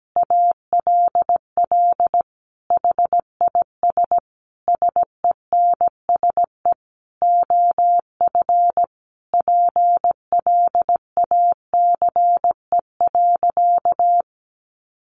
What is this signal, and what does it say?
Morse code, heard as a keyed tone, about 17 wpm, 700 Hz. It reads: ALL HIS SENSE OF PLACE.
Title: Sentences from Top 1400 Words 17wpm